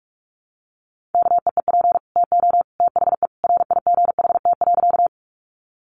Morse code, heard as a keyed tone, about 44 wpm, 700 Hz. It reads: KEEP TO THE RIGHT.
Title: Sentences from Top 300 Words 45wpm